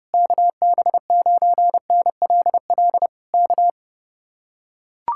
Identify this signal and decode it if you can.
Morse code, heard as a keyed tone, about 30 wpm, 700 Hz. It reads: KB9NLL K